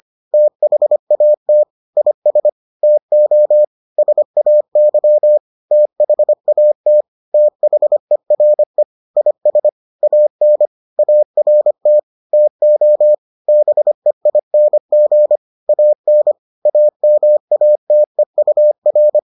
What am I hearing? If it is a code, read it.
Morse code, heard as a keyed tone, about 25 wpm, 600 Hz. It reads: THAT IS TO SAY THAT THERE IS AN ART TO BEING AN AMATEUR